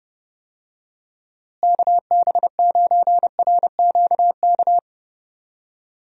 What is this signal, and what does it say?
Morse code, heard as a keyed tone, about 30 wpm, 700 Hz. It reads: KB9RQK
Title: USA Call Signs - 6 Characters - Repeated 2x - 30wpm